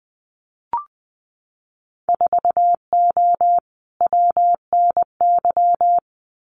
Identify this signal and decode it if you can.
Morse code, heard as a keyed tone, about 20 wpm, 700 Hz. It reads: 4O WNY
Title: Winter Field Day Exchanges Encore 20wpm